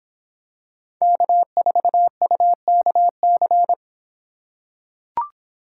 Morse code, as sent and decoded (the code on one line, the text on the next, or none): -.- ....- ..- -.- -.-.
K4UKC